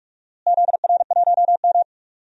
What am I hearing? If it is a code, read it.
Morse code, heard as a keyed tone, about 42 wpm, 700 Hz. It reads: ZR1M
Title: DXCC-100 Call Signs - 4 Characters - Repeated 3x - 45wpm